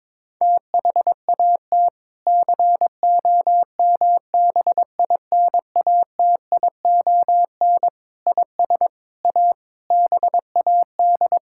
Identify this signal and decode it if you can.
Morse code, heard as a keyed tone, about 22 wpm, 700 Hz. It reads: THAT COMBINATION IS A BAD
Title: Sentences from Top 1000 Words - Review Part 6 - 22wpm